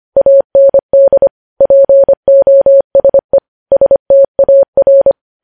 Morse code, heard as a keyed tone, about 25 wpm, 550 Hz. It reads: AND POSE STAR
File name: podcast / News headlines-1x-25x25